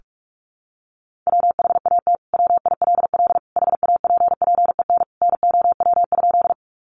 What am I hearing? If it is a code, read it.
Morse code, heard as a keyed tone, about 44 wpm, 700 Hz. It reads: WHAT WILL HAPPEN NOW?
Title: Sentences from Top 400 Words 45wpm